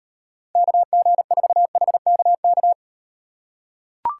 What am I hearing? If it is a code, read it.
Morse code, heard as a keyed tone, about 38 wpm, 700 Hz. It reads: KG4HKK